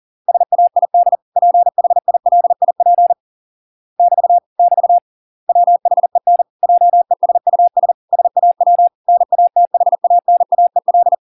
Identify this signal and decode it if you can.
Morse code, heard as a keyed tone, about 40 wpm, 700 Hz. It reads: SAID PHILIP <BT> <BT> WHEN JESUS SAW NATHANAEL